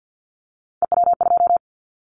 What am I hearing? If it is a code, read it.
Morse code, heard as a keyed tone, about 50 wpm, 700 Hz. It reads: EW2
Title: Callsign Prefixes for Top 50 DXCC - 50wpm